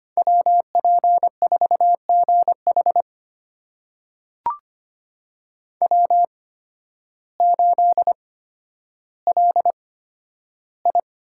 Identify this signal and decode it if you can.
Morse code, heard as a keyed tone, about 25 wpm, spaced as for 12 wpm, 700 Hz. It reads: WP4GH W8LI
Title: USA Call Signs - 8x Character Spacing - 25wpm